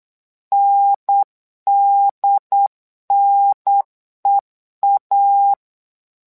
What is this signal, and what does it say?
Morse code, heard as a keyed tone, about 8 wpm, 800 Hz. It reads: NDNEA
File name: LEZIONI / 2 UHDN / 2 UHDN 8wpm2232